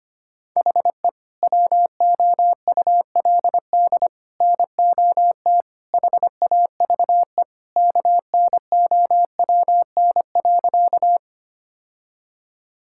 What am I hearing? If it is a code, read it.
Morse code, heard as a keyed tone, about 25 wpm, 700 Hz. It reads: HE WOULD NOT HAVE KNOWN.
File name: practice / 25wpm / Sentences with 3-Word Phrases - 25wpm